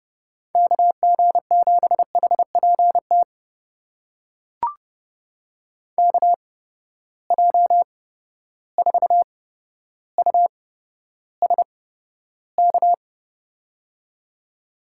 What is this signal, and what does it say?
Morse code, heard as a keyed tone, about 30 wpm, spaced as for 13 wpm, 700 Hz. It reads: KG7HPT KJ4USK
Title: USA Call Signs - 6 Characters - 8x Character Spacing - 30wpm